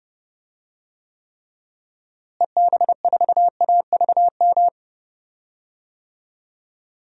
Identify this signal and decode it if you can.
Morse code, heard as a keyed tone, about 30 wpm, 700 Hz. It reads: EB4AVM
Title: DXCC-100 Call Signs - 6 Characters - Repeated 2x - 30wpm